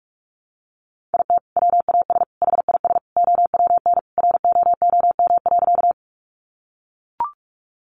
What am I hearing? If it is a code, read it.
Morse code, heard as a keyed tone, about 44 wpm, 700 Hz. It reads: IT WAS HIS OWN ROOM.
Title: Sentences from Top 300 Words 45wpm